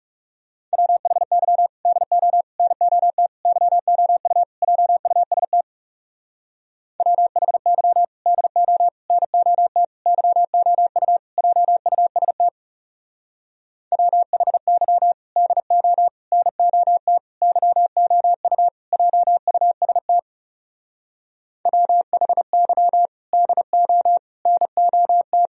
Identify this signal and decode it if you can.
Morse code, heard as a keyed tone, about 39 wpm, 700 Hz. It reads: WHY DO NOT YOU JUST WHY DO NOT YOU JUST WHY DO NOT YOU JUST WHY DO NOT